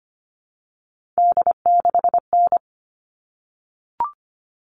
Morse code, heard as a keyed tone, about 25 wpm, 700 Hz. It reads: D6N
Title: All Cut Numbers - 3-Digit 25wpm